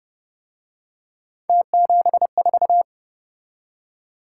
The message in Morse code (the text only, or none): T74